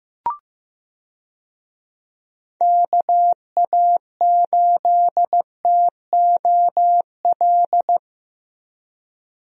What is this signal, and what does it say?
Morse code, heard as a keyed tone, about 15 wpm, 700 Hz. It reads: KA8TOL
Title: USA Call Signs - 6 Characters - Repeated 3x - 15wpm